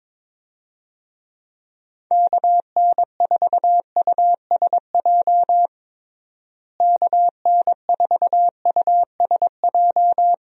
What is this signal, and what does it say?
Morse code, heard as a keyed tone, about 22 wpm, 700 Hz. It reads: KN4USJ KN4USJ
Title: USA Call Signs - 6 Characters - Repeated 3x - 22wpm